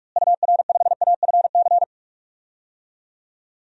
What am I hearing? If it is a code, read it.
Morse code, heard as a keyed tone, about 46 wpm, 700 Hz. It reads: UR5AFC
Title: DXCC-100 Call Signs - 6 Characters - Repeated 2x - 45wpm